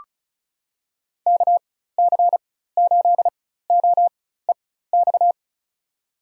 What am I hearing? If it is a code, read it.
Morse code, heard as a keyed tone, about 36 wpm, spaced as for 16 wpm, 700 Hz. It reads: KC8OEX
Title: USA Call Signs - 6 Characters - 4x Character Spacing - 35wpm